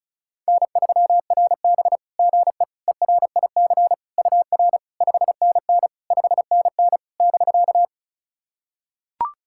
Morse code, heard as a keyed tone, about 36 wpm, 700 Hz. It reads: N3RB GE ERIC UR 5NN 5NN <BK>
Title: SOTA QSO Practice 35wpm